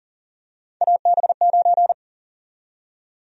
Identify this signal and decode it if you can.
Morse code, heard as a keyed tone, about 40 wpm, 700 Hz. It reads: AB9
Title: USA Call Sign Prefixes - 40wpm